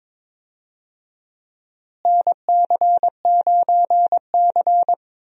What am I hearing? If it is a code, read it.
Morse code, heard as a keyed tone, about 22 wpm, 700 Hz. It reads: NC9C
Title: USA Call Signs - 4 Characters - Repeated 3x - 22wpm